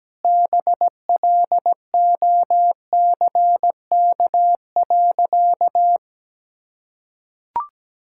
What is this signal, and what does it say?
Morse code, heard as a keyed tone, about 17 wpm, 700 Hz. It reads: BLOCK.
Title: Sentences from Top 1600 Words 17wpm